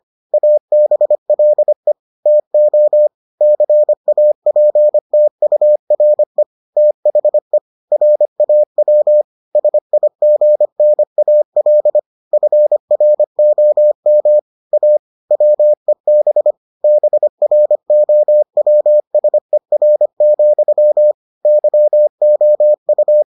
Morse code, heard as a keyed tone, about 25 wpm, 600 Hz. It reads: ABLE TO CAPTURE THE RAW SIGNAL FROM A WEB BROWSER, YOU